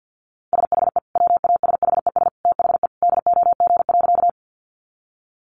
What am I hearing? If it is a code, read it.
Morse code, heard as a keyed tone, about 50 wpm, 700 Hz. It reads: SHE WASHES THE DOG.
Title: Easy Phrases 50wpm